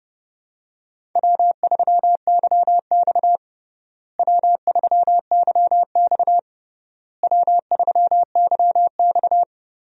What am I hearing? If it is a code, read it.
Morse code, heard as a keyed tone, about 32 wpm, 700 Hz. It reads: W3YX W3YX W3YX
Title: USA Call Signs - 4 Characters - Repeated 3x - 30wpm